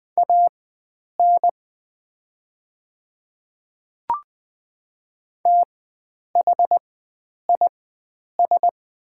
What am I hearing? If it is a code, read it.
Morse code, heard as a keyed tone, about 20 wpm, spaced as for 9 wpm, 690 Hz. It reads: AN THIS
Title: 4x Character Spacing - Top 100 Words - 20wpm